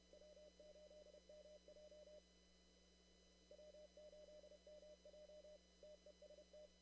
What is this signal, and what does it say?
Morse code, heard as a keyed tone, about 31 wpm, 600 Hz, with noise. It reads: W8MJ W8MJ TEST
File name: CW / kt5j wpx cw-op-k5pi